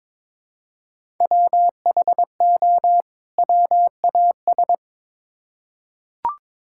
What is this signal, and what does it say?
Morse code, heard as a keyed tone, about 23 wpm, 700 Hz. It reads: WHO WAS
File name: Build A Phrase - 2-Word Phrases - 22wpm